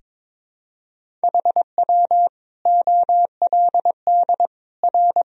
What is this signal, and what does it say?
Morse code, heard as a keyed tone, about 22 wpm, 700 Hz. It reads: HW OLD R
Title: Sentences from Top 200 Words and Abbreviations 22wpm